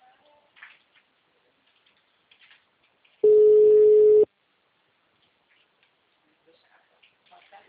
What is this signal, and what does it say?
Ringback tone (Europe, 425 Hz).